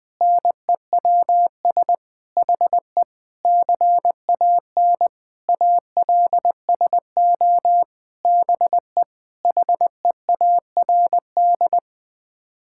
Morse code, heard as a keyed tone, about 19 wpm, 700 Hz. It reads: NEWS HE CAN ALSO BE HEARD
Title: Sets of 9 Words - 20wpm